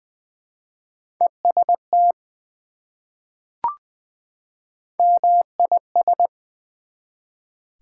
Morse code, heard as a keyed tone, about 20 wpm, 700 Hz. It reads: EST MIS
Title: Prefixes and Suffixes 20wpm